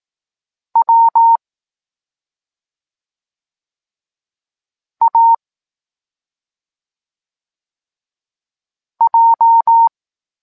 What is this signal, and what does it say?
Morse code, heard as a keyed tone, about 18 wpm, 920 Hz. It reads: W A J